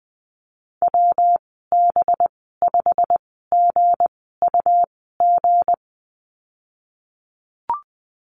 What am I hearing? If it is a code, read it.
Morse code, heard as a keyed tone, about 20 wpm, spaced as for 15 wpm, 700 Hz. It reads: WB5GUG
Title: 219 Lesson 35g - Number 0 - Call Signs 20wpm